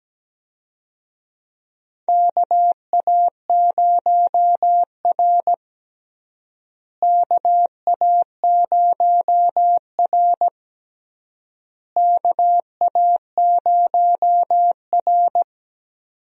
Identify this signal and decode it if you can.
Morse code, heard as a keyed tone, about 17 wpm, 700 Hz. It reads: KA0R KA0R KA0R
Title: USA Call Signs - 4 Characters - Repeated 3x - 17wpm